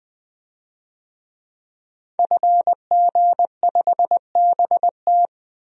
Morse code, heard as a keyed tone, about 20 wpm, 700 Hz. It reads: FG5BT
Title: DXCC-100 Call Signs - 5 Characters - 20wpm